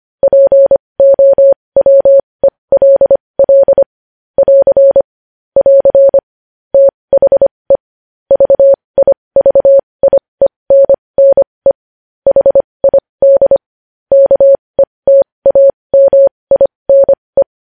Morse code, heard as a keyed tone, about 25 wpm, spaced as for 20 wpm, 550 Hz. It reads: POWELL <AR> <AR> THE VIVIENNE HID KETAMINE
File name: podcast / News headlines-1x-25x20